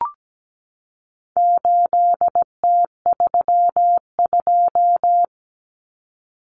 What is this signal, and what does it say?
Morse code, heard as a keyed tone, about 17 wpm, 700 Hz. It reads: 8T32
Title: Common Cut Numbers - 4-Digit 17wpm